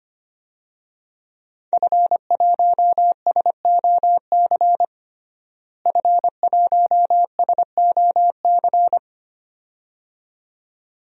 Morse code, heard as a keyed tone, about 25 wpm, 700 Hz. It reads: F1SOC F1SOC
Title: DXCC-100 Call Signs - 5 Characters - Repeated 2x - 25wpm